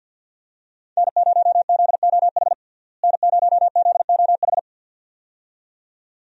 Morse code, heard as a keyed tone, about 49 wpm, 700 Hz. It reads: N0ZOH N0ZOH
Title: USA Call Signs - 5 Characters - Repeated 2x - 50wpm